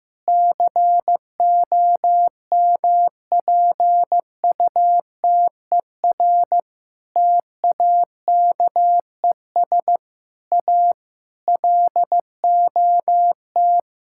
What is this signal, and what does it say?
Morse code, heard as a keyed tone, about 15 wpm, 700 Hz. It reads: COMPUTER TAKES A LOT
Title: Sentences from Top 1000 Words - Review Part 6 - 15wpm